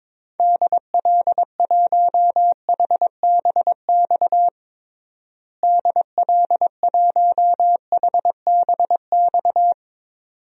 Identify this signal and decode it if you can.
Morse code, heard as a keyed tone, about 23 wpm, 700 Hz. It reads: DL1HBX DL1HBX
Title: DXCC-100 Call Signs - 6 Characters - Repeated 2x - 22wpm